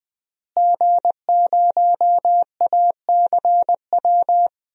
Morse code, heard as a keyed tone, about 20 wpm, 700 Hz. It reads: G0ACW
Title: DXCC-100 Call Signs - 5 Characters - Repeated 3x - 20wpm